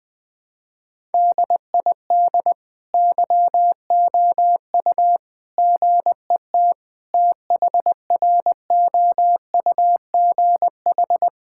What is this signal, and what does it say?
Morse code, heard as a keyed tone, about 20 wpm, 700 Hz. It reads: DID YOU GET THROUGH
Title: Sentences from Top 1000 Words - Review Part 5 - 20wpm